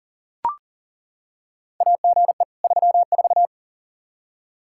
Morse code, heard as a keyed tone, about 40 wpm, 700 Hz. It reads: AGE 34
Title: Practice QSO - Exchange 3 - More Info - 40wpm